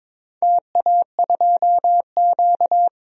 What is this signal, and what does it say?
Morse code, heard as a keyed tone, about 21 wpm, 700 Hz. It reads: TA2Q
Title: DXCC-100 Call Signs - 4 Characters - 22wpm